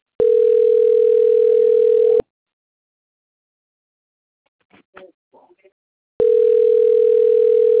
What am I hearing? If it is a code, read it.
Ringback tone (North America), 440 + 480 Hz.